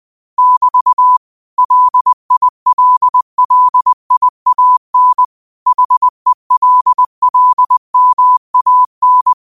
Morse code, heard as a keyed tone, about 21 wpm, 1000 Hz. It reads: <BT> LILLIAN HELLMAN
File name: qotdfiles / 20wpm